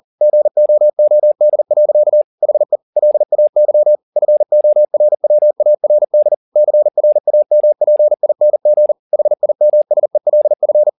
Morse code, heard as a keyed tone, about 40 wpm, 600 Hz. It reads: GOOD. HE LAY FORWARD CRAMPING HIMSELF
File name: the old man and the sea 2 40wpm